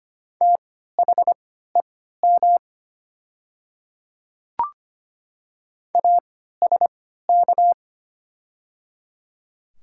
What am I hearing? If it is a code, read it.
Morse code, heard as a keyed tone, about 25 wpm, spaced as for 14 wpm, 690 Hz. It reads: THEM ASK
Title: 3x Character Spacing - Top 200 Words - 25wpm